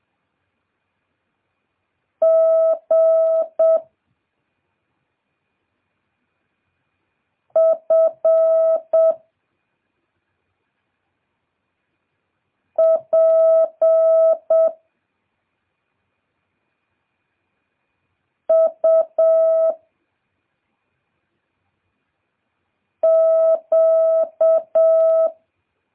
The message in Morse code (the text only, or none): G F P U Q